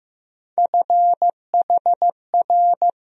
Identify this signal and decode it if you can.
Morse code, heard as a keyed tone, about 15 wpm, 700 Hz. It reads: FHR
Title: USA Call Sign Postfixes - 3 Letters 15wpm